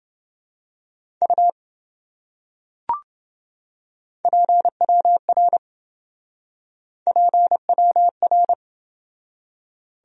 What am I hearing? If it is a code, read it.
Morse code, heard as a keyed tone, about 26 wpm, 700 Hz. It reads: U PWR PWR